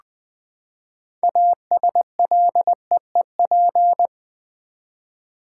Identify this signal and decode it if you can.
Morse code, heard as a keyed tone, about 20 wpm, 700 Hz. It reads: ASLEEP
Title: Six Letter Words - 20wpm